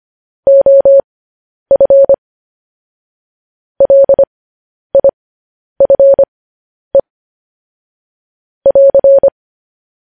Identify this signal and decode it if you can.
Morse code, heard as a keyed tone, about 25 wpm, spaced as for 10 wpm, 550 Hz. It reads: OF LIFE <AR>
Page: news headlines in morse code at 10 wpm